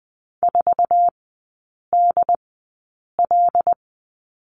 Morse code, heard as a keyed tone, about 20 wpm, spaced as for 8 wpm, 700 Hz. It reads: 4DL
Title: 087 Lesson 15e - Number 5 - Three Characters 20wpm